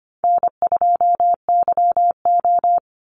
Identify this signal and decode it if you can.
Morse code, heard as a keyed tone, about 25 wpm, 700 Hz. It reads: N2YO
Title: Easy Call Signs Repeated 3x - 25wpm